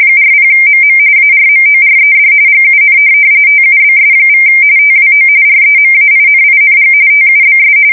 RTTY, about 45.45 baud. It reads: 40MM ARMORED GUN TUBS= TWO REMAINING TUBS
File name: fsk170 / USS-New-Jersey